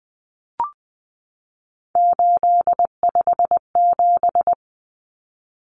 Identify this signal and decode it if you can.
Morse code, heard as a keyed tone, about 20 wpm, 700 Hz. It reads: 857
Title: Numbers - 3-Digit 20wpm